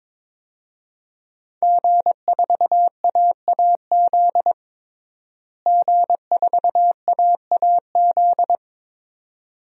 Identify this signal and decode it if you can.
Morse code, heard as a keyed tone, about 22 wpm, 700 Hz. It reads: G4AAZ G4AAZ
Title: DXCC-100 Call Signs - 5 Characters - Repeated 2x - 22wpm